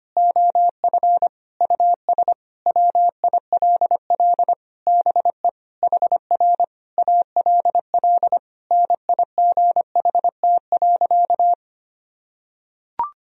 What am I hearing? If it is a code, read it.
Morse code, heard as a keyed tone, about 25 wpm, 700 Hz. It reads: OF US WILL BE HR ALL NIGHT.
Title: Sentences from Top 300 Words and Abbreviations 25wpm